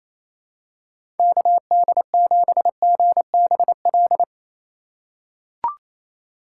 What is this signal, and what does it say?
Morse code, heard as a keyed tone, about 28 wpm, 700 Hz. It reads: KD7GBL
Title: USA Call Signs - 6 Characters - 4x Character Spacing - 28wpm